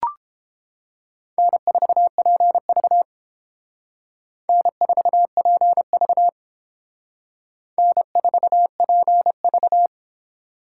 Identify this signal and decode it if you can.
Morse code, heard as a keyed tone, about 31 wpm, 700 Hz. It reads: N4PV N4PV N4PV